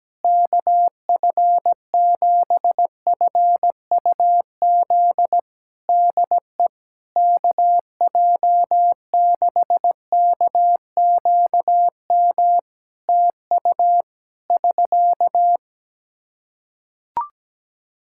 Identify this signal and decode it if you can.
Morse code, heard as a keyed tone, about 17 wpm, 700 Hz. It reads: KF7FUZ DE KJ6KQM TU <SK>
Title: Practice QSO - Exchange 4 - Ending - 17wpm